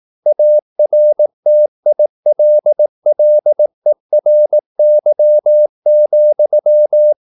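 Morse code, heard as a keyed tone, about 18 wpm, 600 Hz. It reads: ARTILLERY,
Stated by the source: Morse practice at 18 wpm, English text, from CW med Gnister.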